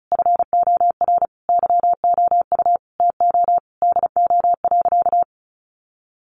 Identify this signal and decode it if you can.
Morse code, heard as a keyed tone, about 34 wpm, 700 Hz. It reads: FOR YOU TO DO.
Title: Sentences from Top 1300 Words 35wpm